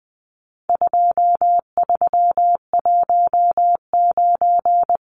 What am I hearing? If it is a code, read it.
Morse code, heard as a keyed tone, about 20 wpm, 700 Hz. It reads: 2319
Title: Numbers - 4-Digit 20wpm